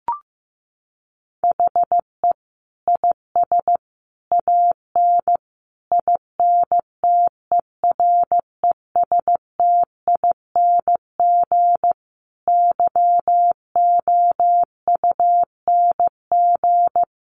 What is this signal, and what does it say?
Morse code, heard as a keyed tone, about 15 wpm, 700 Hz. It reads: HE IS AN INTERESTING YOUNG
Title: Sentences from Top 1000 Words 15wpm